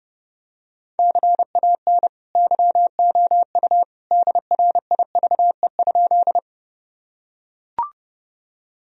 Morse code, heard as a keyed tone, about 30 wpm, 700 Hz. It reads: CAN YOU DRIVE?
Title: Sentences from Top 1000 Words - Review Part 2 - 30wpm